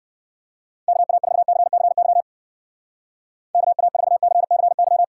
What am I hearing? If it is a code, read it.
Morse code, heard as a keyed tone, about 59 wpm, 700 Hz. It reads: KA4CCY KA4CCY